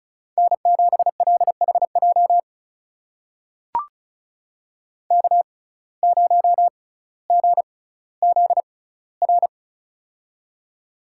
Morse code, heard as a keyed tone, about 36 wpm, spaced as for 18 wpm, 700 Hz. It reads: N7LHJ K0GZR
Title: USA Call Signs - 5 Characters - 6x Character Spacing - 35wpm